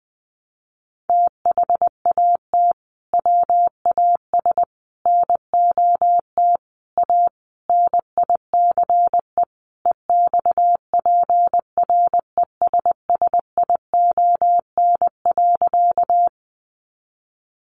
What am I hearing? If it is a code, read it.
Morse code, heard as a keyed tone, about 20 wpm, 700 Hz. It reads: THAT WAS NOT A NICE EXPRESSION.
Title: Sentences from Top 1200 Words 20wpm